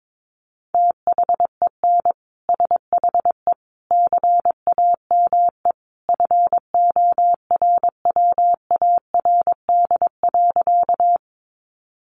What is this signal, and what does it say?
Morse code, heard as a keyed tone, about 22 wpm, 700 Hz. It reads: THEN SHE CAME FORWARD.
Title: Sentences from Top 1300 Words 22wpm